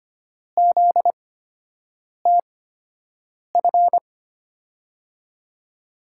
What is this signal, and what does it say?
Morse code, heard as a keyed tone, about 25 wpm, spaced as for 7 wpm, 700 Hz. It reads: ZTF
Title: USA Call Signs - 5 Characters - 8x Character Spacing - 25wpm